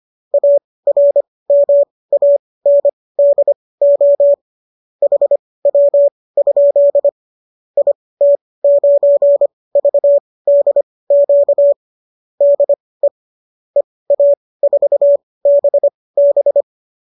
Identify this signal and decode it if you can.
Morse code, heard as a keyed tone, about 25 wpm, spaced as for 18 wpm, 570 Hz. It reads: ARMANDO HW? IT9VDQ DE EA4BB